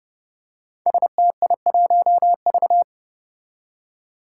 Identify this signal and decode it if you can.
Morse code, heard as a keyed tone, about 30 wpm, 700 Hz. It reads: STI1V
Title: Five Challenging Letter-Number Combinations 30wpm